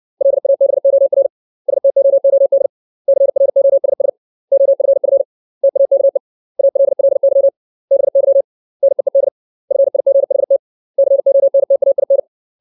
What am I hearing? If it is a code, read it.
Morse code, heard as a keyed tone, about 60 wpm, 550 Hz. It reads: LABOR STOOD CROSS OFF TAKE ALLY BY NEED FIGHT CONTAIN
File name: 632 words / 632 Words 60WPM